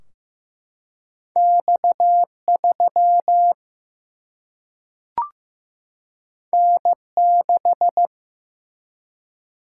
Morse code, heard as a keyed tone, about 15 wpm, 700 Hz. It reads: X 3 N 6
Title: Single-Letter Single-Number 15wpm